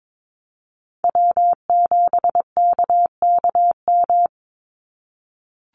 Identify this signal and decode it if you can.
Morse code, heard as a keyed tone, about 22 wpm, 700 Hz. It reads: W7KKM
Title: NA SOTA Chaser Callsigns 22wpm